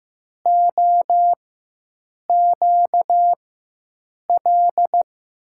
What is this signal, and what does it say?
Morse code, heard as a keyed tone, about 15 wpm, spaced as for 7 wpm, 700 Hz. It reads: OQL
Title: USA Call Signs - 4x Character Spacing - 15wpm